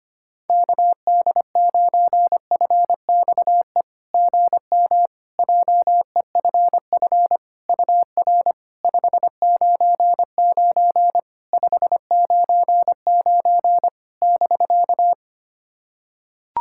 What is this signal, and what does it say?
Morse code, heard as a keyed tone, about 25 wpm, 700 Hz. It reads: KD9FXE GM JEFF UR 599 599 <BK>
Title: SOTA QSO Practice 25wpm